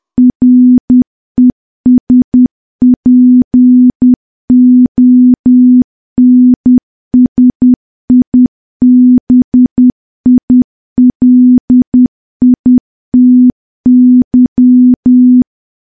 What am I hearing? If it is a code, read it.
Morse code, heard as a keyed tone, about 10 wpm, 250 Hz. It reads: RESPONSIBILITY
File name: morse / morse-training-10WPM